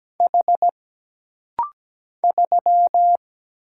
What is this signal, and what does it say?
Morse code, heard as a keyed tone, about 17 wpm, 700 Hz. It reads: H 3
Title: Single Letter-Number - Rapid-Fire 17wpm